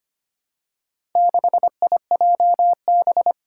Morse code, heard as a keyed tone, about 25 wpm, 700 Hz. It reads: 6IJB
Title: Four Challenging Letter-Number Combinations 25wpm